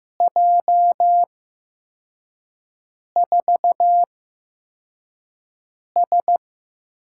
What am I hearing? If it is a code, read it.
Morse code, heard as a keyed tone, about 15 wpm, spaced as for 4 wpm, 700 Hz. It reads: J4S
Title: USA Call Signs - 4 Characters - 8x Character Spacing - 15wpm